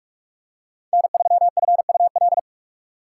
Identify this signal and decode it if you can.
Morse code, heard as a keyed tone, about 44 wpm, 700 Hz. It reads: N3FUL